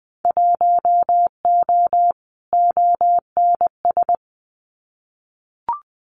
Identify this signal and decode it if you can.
Morse code, heard as a keyed tone, about 20 wpm, 700 Hz. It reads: 1O ONS
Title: Winter Field Day Exchanges 20wpm